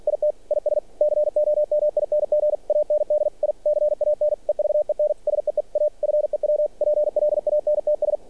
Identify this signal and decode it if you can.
Morse code, heard as a keyed tone, about 46 wpm, 600 Hz. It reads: IT IS COMING AND I CAN EVEN SEE A FEW PLANTS
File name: archived / CRSnet